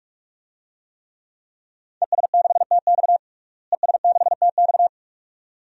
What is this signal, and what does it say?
Morse code, heard as a keyed tone, about 46 wpm, 700 Hz. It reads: ES6TX ES6TX
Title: DXCC-100 Call Signs - 5 Characters - Repeated 2x - 45wpm